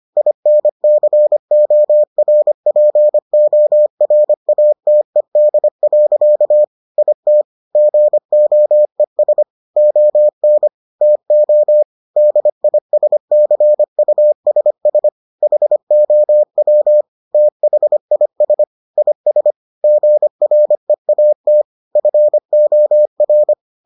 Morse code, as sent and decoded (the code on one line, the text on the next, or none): .. -. -.-. --- .-. .--. --- .-. .- - . -.. .-.-.- / .. - / --. --- . ... / --- -. / - --- / -.. .. ... -.-. ..- ... ... / .... --- .-- / - .... .. ... / .. ... / --. .-. .- - / ..-. --- .-.
INCORPORATED. IT GOES ON TO DISCUSS HOW THIS IS GREAT FOR